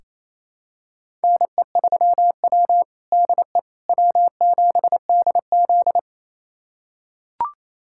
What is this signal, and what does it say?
Morse code, heard as a keyed tone, about 28 wpm, 700 Hz. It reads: NE3W DE W7DZ
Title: USA Callsign DE Callsign - 4 Characters - 28wpm